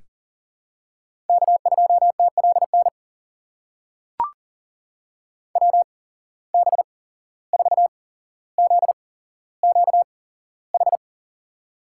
Morse code, heard as a keyed tone, about 40 wpm, spaced as for 16 wpm, 700 Hz. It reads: K2TLN WB4ZQH
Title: USA Call Signs - 8x Character Spacing - 40wpm